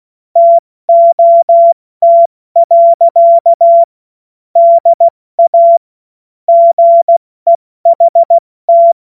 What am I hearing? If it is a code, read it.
Morse code, heard as a keyed tone, about 16 wpm, 680 Hz. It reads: TOT. DA GEHT
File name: Morsekurs / Witze / Witz 28. September 2021, 70 F80